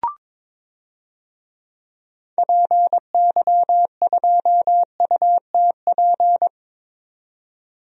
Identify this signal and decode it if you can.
Morse code, heard as a keyed tone, about 22 wpm, 700 Hz. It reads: PY2UTP
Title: DXCC-100 Call Signs - 6 Characters - 22wpm